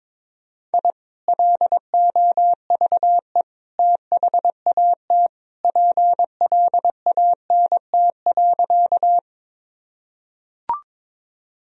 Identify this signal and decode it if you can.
Morse code, heard as a keyed tone, about 21 wpm, 700 Hz. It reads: I LOVE THAT PLANT.
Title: Sentences from Top 1000 Words - Review Part 1 - 22wpm